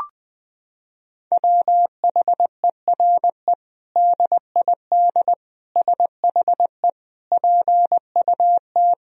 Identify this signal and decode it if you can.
Morse code, heard as a keyed tone, about 20 wpm, 700 Hz. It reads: WHERE DID SHE PUT
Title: Sentences from Top 1000 Words - Review Part 2 - 20wpm